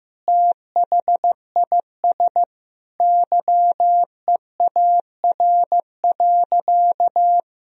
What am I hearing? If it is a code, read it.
Morse code, heard as a keyed tone, about 15 wpm, 700 Hz. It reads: THIS YEAR.
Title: Sentences from Top 1000 Words - Review Part 1 - 15wpm